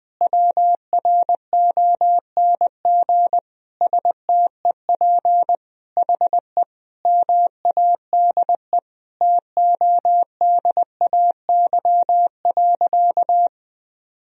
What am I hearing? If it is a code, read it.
Morse code, heard as a keyed tone, about 20 wpm, 700 Hz. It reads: WRONG STEP HE MADE TODAY.
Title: Sentences from Top 1000 Words - Review Part 2 - 20wpm